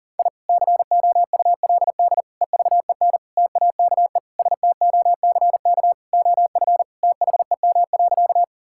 Morse code, heard as a keyed tone, about 40 wpm, 690 Hz. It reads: I COULD EVEN TAKE STOCK OF THEM.
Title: Sentences from Top 1000 Words - Review Part 5 - 40wpm